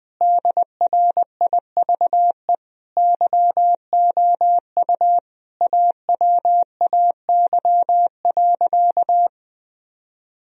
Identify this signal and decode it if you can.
Morse code, heard as a keyed tone, about 20 wpm, 700 Hz. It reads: DRIVE YOU AWAY.
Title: Sentences from Top 1000 Words - Review Part 2 - 20wpm